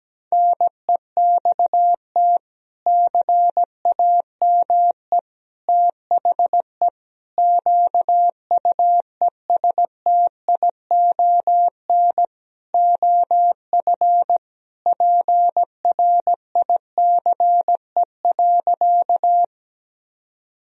Morse code, heard as a keyed tone, about 17 wpm, 700 Hz. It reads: NEXT CAME THE QUESTION OF PRICE.
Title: Sentences from Top 1000 Words - Review Part 4 - 17wpm